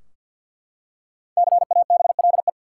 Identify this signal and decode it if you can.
Morse code, heard as a keyed tone, about 52 wpm, 700 Hz. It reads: CABLE
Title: Five Letter Words - 50wpm